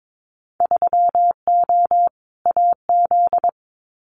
Morse code, heard as a keyed tone, about 22 wpm, 700 Hz. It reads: 3O AZ
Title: Winter Field Day Exchanges 22wpm